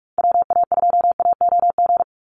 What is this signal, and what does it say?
Morse code, heard as a keyed tone, about 44 wpm, 700 Hz. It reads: WA2AOG
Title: Call Signs Repeated 2x - 45wpm